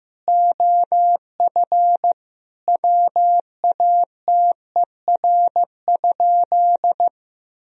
Morse code, heard as a keyed tone, about 15 wpm, 700 Hz. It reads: OF WATER?
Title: Sentences from Top 1000 Words - Review Part 3 - 15wpm